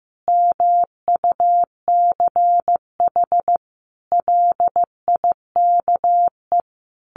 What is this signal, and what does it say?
Morse code, heard as a keyed tone, about 15 wpm, 700 Hz. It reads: MUCH LIKE
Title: Sentences from Top 1300 Words 15wpm